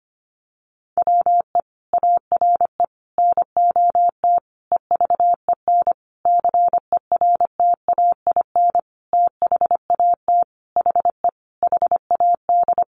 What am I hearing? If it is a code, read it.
Morse code, heard as a keyed tone, about 25 wpm, 700 Hz. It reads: WE ARE NOT EVEN CERTAIN THAT HE HAD
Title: Sentences from Top 1500 Words 25wpm